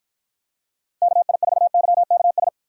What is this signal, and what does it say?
Morse code, heard as a keyed tone, about 51 wpm, 700 Hz. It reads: KI4YKS